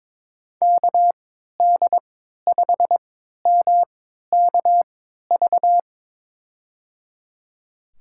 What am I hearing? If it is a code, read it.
Morse code, heard as a keyed tone, about 22 wpm, spaced as for 12 wpm, 700 Hz. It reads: KD5MKV
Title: USA Call Signs - 6 Characters - 3x Character Spacing - 22wpm